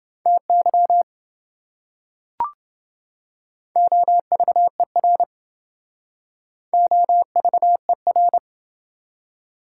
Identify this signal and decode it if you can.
Morse code, heard as a keyed tone, about 28 wpm, 700 Hz. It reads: TY OVER OVER